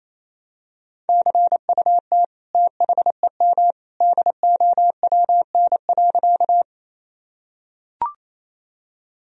Morse code, heard as a keyed tone, about 27 wpm, 700 Hz. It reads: CUT THEM DOWN.
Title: Sentences from Top 1000 Words - Review Part 1 - 28wpm